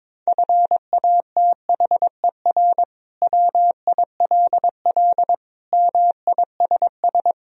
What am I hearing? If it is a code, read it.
Morse code, heard as a keyed tone, about 22 wpm, 700 Hz. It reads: FATHER WILL MISS
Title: Sentences from Top 1000 Words - Review Part 3 - 22wpm